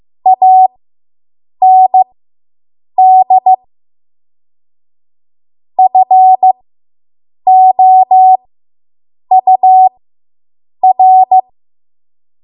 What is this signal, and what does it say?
Morse code, heard as a keyed tone, about 15 wpm, spaced as for 7 wpm, 750 Hz. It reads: AND FOUR